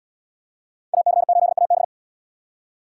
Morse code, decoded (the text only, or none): AC8AB